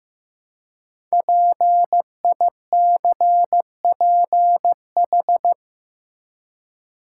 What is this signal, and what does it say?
Morse code, heard as a keyed tone, about 15 wpm, 700 Hz. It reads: PICPH
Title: Code Groups - 5 Letters - 15wpm